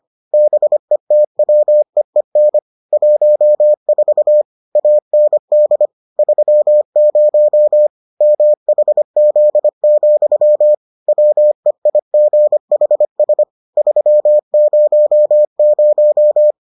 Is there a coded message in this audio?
Yes: Morse code from 0.3 to 16.5 s, heard as a keyed tone, about 25 wpm, 600 Hz. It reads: BETWEEN 14 AND 30 MHZ, WEIGHS 300